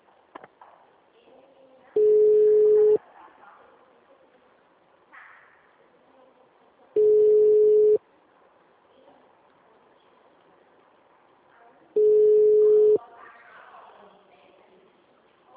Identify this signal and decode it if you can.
Ringback tone (Europe, 425 Hz).
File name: out- -2008